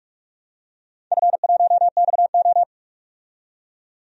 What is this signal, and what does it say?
Morse code, heard as a keyed tone, about 44 wpm, 700 Hz. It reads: F1XO
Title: DXCC-100 Call Signs - 4 Characters - 45wpm